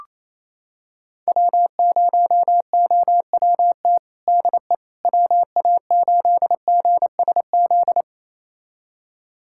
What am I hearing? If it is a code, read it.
Morse code, heard as a keyed tone, about 27 wpm, 700 Hz. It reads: W0OWT DE WA8GSZ
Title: USA Callsign DE Callsign 28wpm